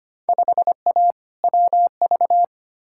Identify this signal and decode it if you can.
Morse code, heard as a keyed tone, about 25 wpm, 700 Hz. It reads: 5A WV
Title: Field Day Exchanges - 25wpm